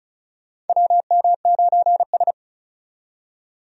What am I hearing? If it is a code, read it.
Morse code, heard as a keyed tone, about 36 wpm, 700 Hz. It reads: WM9S
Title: USA Call Signs - 4 Characters - 35wpm